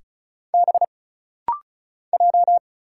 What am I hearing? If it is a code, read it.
Morse code, heard as a keyed tone, about 35 wpm, 700 Hz. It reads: B J
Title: Single Letters - Rapid-Fire 35wpm